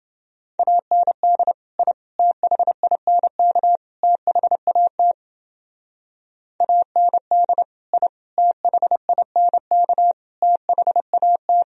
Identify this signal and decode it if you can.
Morse code, heard as a keyed tone, about 29 wpm, 700 Hz. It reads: AND I THINK THAT AND I THINK THAT